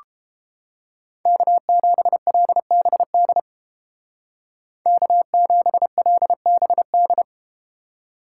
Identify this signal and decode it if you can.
Morse code, heard as a keyed tone, about 31 wpm, 700 Hz. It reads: K7LBD K7LBD